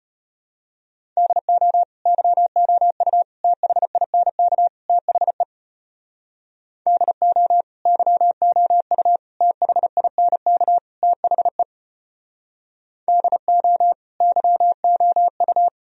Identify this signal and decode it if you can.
Morse code, heard as a keyed tone, about 33 wpm, 700 Hz. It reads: DO YOU THINK THE DO YOU THINK THE DO YOU